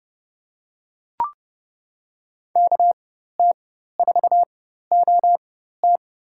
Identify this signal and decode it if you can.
Morse code, heard as a keyed tone, about 29 wpm, spaced as for 14 wpm, 700 Hz. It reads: KT4OT
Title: USA Call Signs - 5 Characters - 4x Character Spacing - 30wpm